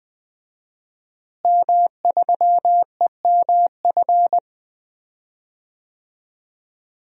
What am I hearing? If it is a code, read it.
Morse code, heard as a keyed tone, about 20 wpm, 700 Hz. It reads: M3EMF
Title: DXCC-100 Call Signs - 5 Characters - Repeated 3x - 20wpm